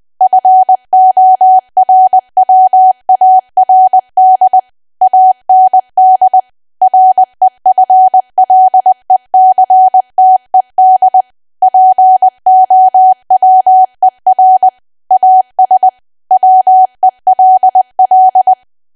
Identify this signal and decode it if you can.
Morse code, heard as a keyed tone, about 20 wpm, 750 Hz. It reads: FORWARD AND REFLECTED POWER AS WELL
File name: cwa / W1AW-Level3 / 20WPM